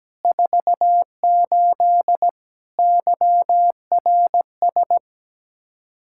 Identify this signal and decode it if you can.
Morse code, heard as a keyed tone, about 17 wpm, 700 Hz. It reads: 48 YRS
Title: K1USN SST QSO Practice 17wpm